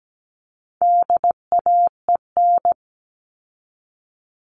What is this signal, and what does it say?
Morse code, heard as a keyed tone, about 17 wpm, 700 Hz. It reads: DAEN
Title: All Cut Numbers - 4-Digit 17wpm